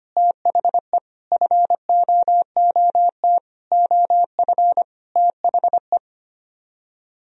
Morse code, heard as a keyed tone, about 32 wpm, 700 Hz. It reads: THE FOOT OF THE